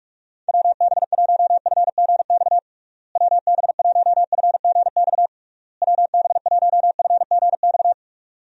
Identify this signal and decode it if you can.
Morse code, heard as a keyed tone, about 47 wpm, 700 Hz. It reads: WB1FGX WB1FGX WB1FGX
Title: USA Call Signs - 6 Characters - Repeated 3x - 45wpm